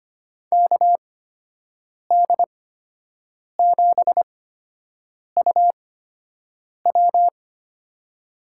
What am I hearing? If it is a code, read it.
Morse code, heard as a keyed tone, about 25 wpm, spaced as for 7 wpm, 700 Hz. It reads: KD7UW
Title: USA Call Signs - 8x Character Spacing - 25wpm